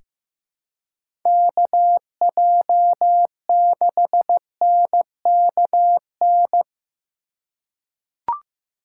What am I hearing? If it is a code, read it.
Morse code, heard as a keyed tone, about 15 wpm, 700 Hz. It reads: KJ6NKN
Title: USA Call Signs - 6 Characters - 3x Character Spacing - 15wpm